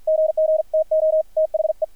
Morse code, heard as a keyed tone, about 50 wpm, 630 Hz, with radio noise. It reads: GGTO THE
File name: archived / CRSnet